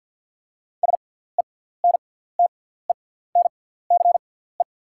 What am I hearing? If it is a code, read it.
Morse code, heard as a keyed tone, about 48 wpm, spaced as for 17 wpm, 690 Hz. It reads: SENTENCE
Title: 6x Character Spacing - Top 300 Words - 50wpm